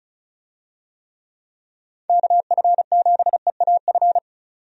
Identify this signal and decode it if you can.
Morse code, heard as a keyed tone, about 35 wpm, 700 Hz. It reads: KF7EAF
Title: USA Call Signs - 6 Characters - 35wpm